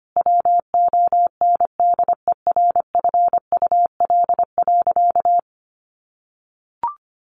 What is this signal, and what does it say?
Morse code, heard as a keyed tone, about 25 wpm, 700 Hz. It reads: WONDERFUL.
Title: Sentences from Top 1100 Words 25wpm